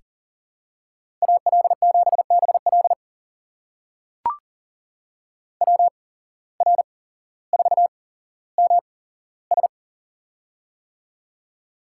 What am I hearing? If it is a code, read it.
Morse code, heard as a keyed tone, about 40 wpm, spaced as for 17 wpm, 700 Hz. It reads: AL7BL WR4MS
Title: USA Call Signs - 5 Characters - 8x Character Spacing - 40wpm